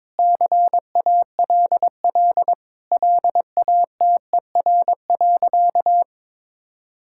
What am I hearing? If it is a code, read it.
Morse code, heard as a keyed tone, about 22 wpm, 700 Hz. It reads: CALL LATER.
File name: practice / 22wpm / Sentences with 4-Word Phrases - 22wpm